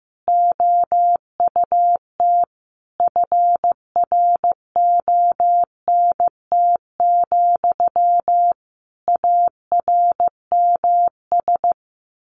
Morse code, heard as a keyed tone, about 14 wpm, 700 Hz. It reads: OUT FRONT, ARMS
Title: A Sound Of Thunder 15wpm